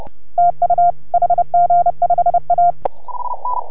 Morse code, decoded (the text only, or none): TU HG5A